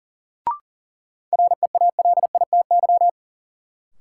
Morse code, heard as a keyed tone, about 38 wpm, 700 Hz. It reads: REALITY
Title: Sentences from Top 1200 Words - New Word Drill - 40wpm